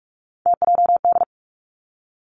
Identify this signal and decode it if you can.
Morse code, heard as a keyed tone, about 46 wpm, 700 Hz. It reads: TJD
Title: Three Challenging Letter-Number Combinations 45wpm